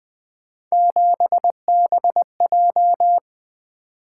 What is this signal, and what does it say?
Morse code, heard as a keyed tone, about 20 wpm, 700 Hz. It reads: 7BJ
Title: Three Letter-Number Combinations - 20wpm